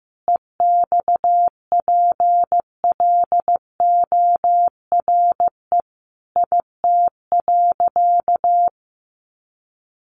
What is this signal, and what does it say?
Morse code, heard as a keyed tone, about 15 wpm, 700 Hz. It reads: EXPLORE IT.
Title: Sentences from Top 1100 Words 15wpm